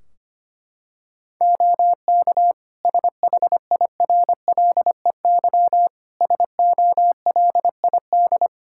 Morse code, heard as a keyed tone, about 25 wpm, 700 Hz. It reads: OK SHIRLEY SOLID
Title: Practice QSO - Exchange 3 - More Info - 25wpm